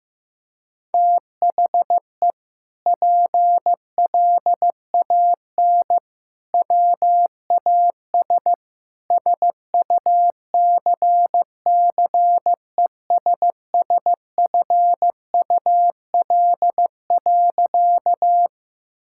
Morse code, heard as a keyed tone, about 15 wpm, 700 Hz. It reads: THE PLAN WAS SUCCESSFUL.
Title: Sentences from Top 1000 Words - Review Part 6 - 15wpm